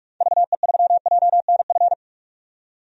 Morse code, heard as a keyed tone, about 46 wpm, 700 Hz. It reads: VE3JNF
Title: DXCC-100 Call Signs - 6 Characters - 45wpm